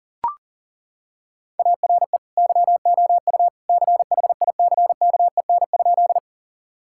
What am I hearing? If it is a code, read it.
Morse code, heard as a keyed tone, about 40 wpm, 700 Hz. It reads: ARE YOU CHICKEN?
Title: Sentences from Top 1000 Words - Review Part 5 - 40wpm